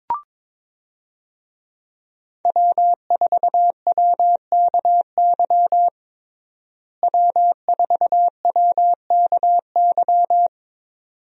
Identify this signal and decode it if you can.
Morse code, heard as a keyed tone, about 21 wpm, 700 Hz. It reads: W4WKY W4WKY
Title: USA Call Signs - 5 Characters - Repeated 2x - 22wpm